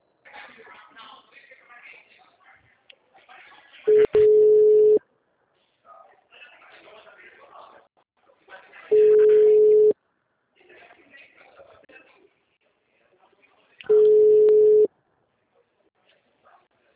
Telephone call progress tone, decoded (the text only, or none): ringback tone